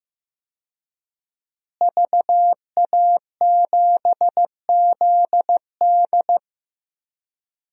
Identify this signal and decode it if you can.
Morse code, heard as a keyed tone, about 15 wpm, 700 Hz. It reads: VA7ZD
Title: DXCC-100 Call Signs - 5 Characters - Repeated 3x - 15wpm